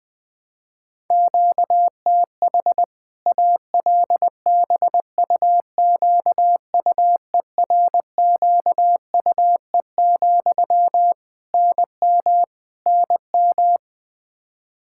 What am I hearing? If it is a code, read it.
Morse code, heard as a keyed tone, about 20 wpm, 700 Hz. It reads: QTH ALBUQUERQUE, NM NM
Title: K1USN SST QSO Practice 20wpm